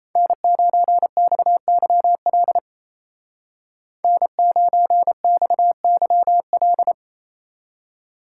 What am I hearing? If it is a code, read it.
Morse code, heard as a keyed tone, about 37 wpm, 700 Hz. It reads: N9XYL N9XYL